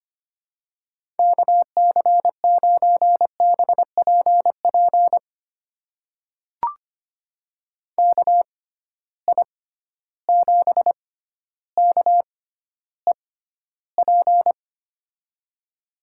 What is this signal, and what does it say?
Morse code, heard as a keyed tone, about 25 wpm, spaced as for 13 wpm, 700 Hz. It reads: KC9BPP KI7KEP
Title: USA Call Signs - 6 Characters - 6x Character Spacing - 25wpm